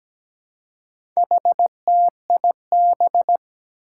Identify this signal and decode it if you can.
Morse code, heard as a keyed tone, about 17 wpm, 700 Hz. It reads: HTIB
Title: Four Challenging Letter-Number Combinations 17wpm